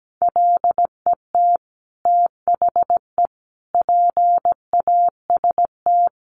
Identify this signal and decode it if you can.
Morse code, heard as a keyed tone, about 17 wpm, 700 Hz. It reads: LET THE PAST
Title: Sentences from Top 1200 Words 17wpm